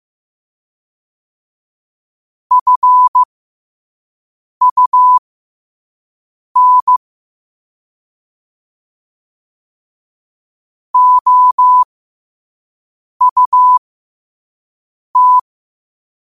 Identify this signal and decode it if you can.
Morse code, heard as a keyed tone, about 15 wpm, spaced as for 5 wpm, 1000 Hz. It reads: FUN OUT
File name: qotdfiles / 5wpm